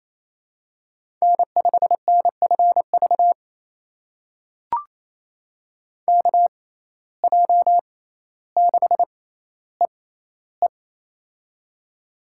Morse code, heard as a keyed tone, about 28 wpm, spaced as for 14 wpm, 700 Hz. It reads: N5NFV KJ6EE
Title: USA Call Signs - 5 Characters - 6x Character Spacing - 28wpm